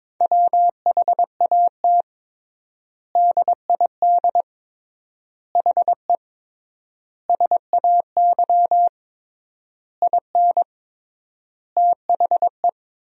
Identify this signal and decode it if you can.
Morse code, heard as a keyed tone, about 22 wpm, 700 Hz. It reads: WHAT DID HE SAY IN THE
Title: 3x Word-Spacing - Sentences from Top 300 Words - 22wpm